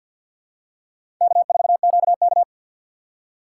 Morse code, heard as a keyed tone, about 52 wpm, 700 Hz. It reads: K4QK
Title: USA Call Signs - 3x Character Spacing - 50wpm